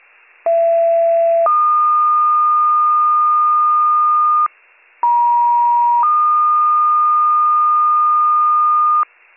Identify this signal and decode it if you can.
Two-tone page, heard as a one-second tone, then a three-second tone, about 670 then 1220 Hz, about 950 then 1220 Hz, with noise. It is heard from 0.5 to 9.0 s.